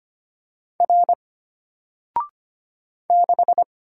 Morse code, heard as a keyed tone, about 25 wpm, 700 Hz. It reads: R 6
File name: Single-Letter-Number-25wpm